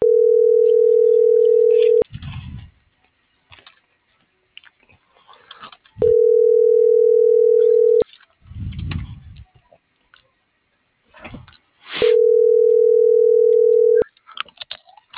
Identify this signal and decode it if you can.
Ringback tone (North America), 440 + 480 Hz.
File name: CENTRAL MABE / out luis.palacios